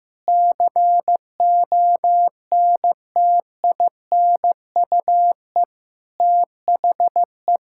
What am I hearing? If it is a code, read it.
Morse code, heard as a keyed tone, about 15 wpm, 700 Hz. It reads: CONTINUE THE
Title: Sentences from Top 1000 Words - Review Part 6 - 15wpm